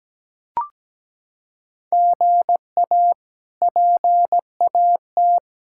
Morse code, heard as a keyed tone, about 17 wpm, 700 Hz. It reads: GA PAT
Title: K1USN SST QSO Practice - 17wpm